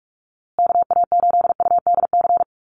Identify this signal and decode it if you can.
Morse code, heard as a keyed tone, about 44 wpm, 700 Hz. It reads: KA8UDC
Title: USA Call Signs Repeated 2x 45wpm